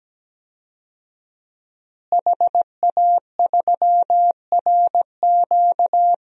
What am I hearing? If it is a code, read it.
Morse code, heard as a keyed tone, about 17 wpm, 700 Hz. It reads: HA3RQ
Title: DXCC-100 Call Signs - 5 Characters - Repeated 2x - 17wpm